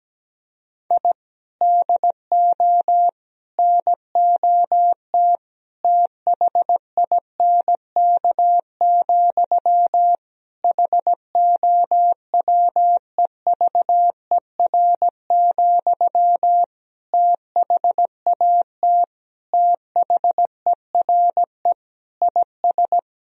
Morse code, heard as a keyed tone, about 17 wpm, 700 Hz. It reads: I DO NOT THINK, HOWEVER, THAT THERE IS
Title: Sentences from Top 1000 Words - Review Part 3 - 17wpm